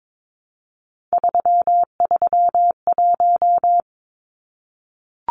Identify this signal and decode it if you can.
Morse code, heard as a keyed tone, about 22 wpm, 700 Hz. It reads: 331
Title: Numbers - 3-Digit 22wpm